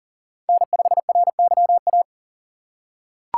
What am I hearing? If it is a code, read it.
Morse code, heard as a keyed tone, about 40 wpm, 700 Hz. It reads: N5RYA
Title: USA Call Signs - 6x Character Spacing - 40wpm